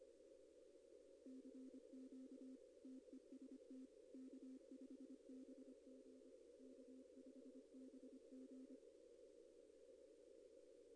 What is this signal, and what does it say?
Morse code, heard as a keyed tone, about 25 wpm, 260 Hz. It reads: CQ TEST K5BG K5BG
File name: contests / arrl-160-k5tr-op-w5tn -2200